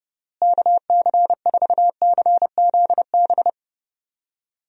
Morse code, heard as a keyed tone, about 30 wpm, 700 Hz. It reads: KC4CZB
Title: USA Call Signs - 6 Characters - Repeated 2x - 30wpm